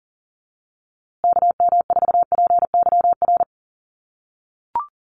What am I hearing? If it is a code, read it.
Morse code, heard as a keyed tone, about 40 wpm, 700 Hz. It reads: KM4PYR
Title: USA Call Signs Repeated 2x 40wpm